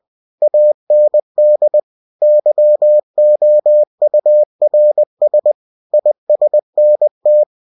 Morse code, heard as a keyed tone, about 20 wpm, 600 Hz. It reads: AND YOURS ISNT
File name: cw-books / winnie-the-pooh / winnie-20wpm-page-0027